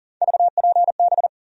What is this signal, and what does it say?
Morse code, heard as a keyed tone, about 40 wpm, 700 Hz. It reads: VPB
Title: USA Call Sign Postfixes - 3 Letters 40wpm